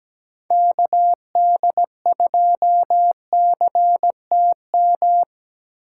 Morse code, heard as a keyed tone, about 20 wpm, 700 Hz. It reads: KD2CTM
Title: Speed Racing - USA 2x3 Call Signs 17wpm